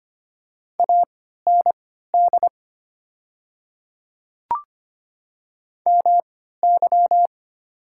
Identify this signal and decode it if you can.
Morse code, heard as a keyed tone, about 25 wpm, spaced as for 14 wpm, 700 Hz. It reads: AND MY
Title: 3x Character Spacing - Top 100 Words - 25wpm